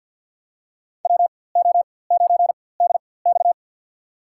Morse code, heard as a keyed tone, about 47 wpm, spaced as for 23 wpm, 700 Hz. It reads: WO9DX
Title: USA Call Signs - 5 Characters - 4x Character Spacing - 50wpm